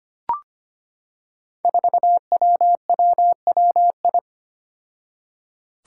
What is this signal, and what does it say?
Morse code, heard as a keyed tone, about 25 wpm, 700 Hz. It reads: 4WWWI
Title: Five Challenging Letter-Number Combinations 25wpm